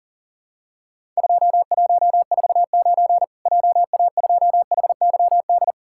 Morse code, heard as a keyed tone, about 37 wpm, 700 Hz. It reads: 2149 JA2HYD
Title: Worked All Europe DX Contest - QTC Exchange - 40wpm